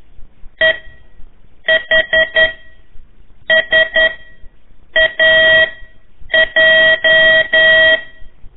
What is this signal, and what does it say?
Morse code, heard as a keyed tone, about 11 wpm, spaced as for 7 wpm, 1900 Hz. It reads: EHSAJ
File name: 3 lezione CW 1 AIR